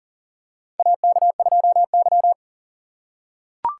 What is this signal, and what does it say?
Morse code, heard as a keyed tone, about 37 wpm, 700 Hz. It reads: AK2Y
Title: USA Call Signs - 4 Characters - 4x Character Spacing - 40wpm